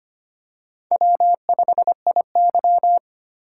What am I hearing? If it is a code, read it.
Morse code, heard as a keyed tone, about 25 wpm, 700 Hz. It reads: W5IY
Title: USA Call Signs - 4x Character Spacing - 25wpm